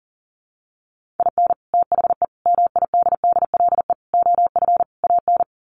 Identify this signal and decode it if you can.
Morse code, heard as a keyed tone, about 37 wpm, 700 Hz. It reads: IN THE MIDDLE OF AN
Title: Sets of 5 Words Encore2 40wpm